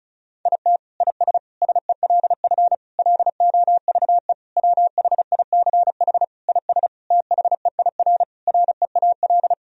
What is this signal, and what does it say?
Morse code, heard as a keyed tone, about 34 wpm, 700 Hz. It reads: IT IS SELF LOVE WHICH IS THEIR REAL
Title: Sentences from Top 1000 Words - Review Part 1 - 35wpm